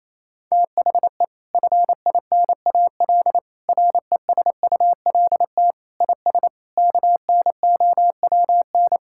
Morse code, heard as a keyed tone, about 28 wpm, 700 Hz. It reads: THE FINAL RESULT IS KNOWN
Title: Sentences from Top 1000 Words - Review Part 5 - 28wpm